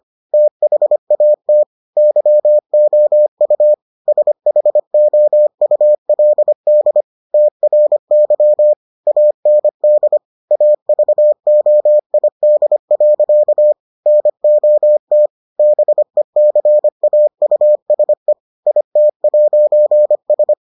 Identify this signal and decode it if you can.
Morse code, heard as a keyed tone, about 25 wpm, 600 Hz. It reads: THAT YOU SHOULD TRY AND AVOID. NOT BECAUSE IT'S